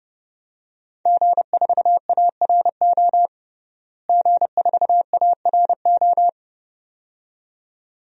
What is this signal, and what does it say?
Morse code, heard as a keyed tone, about 32 wpm, 700 Hz. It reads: G4ARO G4ARO
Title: DXCC-100 Call Signs - 5 Characters - Repeated 2x - 30wpm